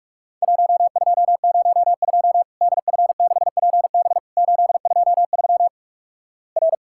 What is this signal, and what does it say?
Morse code, heard as a keyed tone, about 48 wpm, 700 Hz. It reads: 1202 DF6PB 823 R
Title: Worked All Europe DX Contest - QTC Exchange - 45wpm